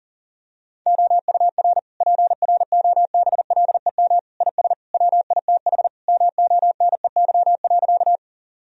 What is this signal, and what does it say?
Morse code, heard as a keyed tone, about 37 wpm, 700 Hz. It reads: OUR PROBLEM IS WITH MONEY.
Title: Sentences from Top 1000 Words - Review Part 2 - 40wpm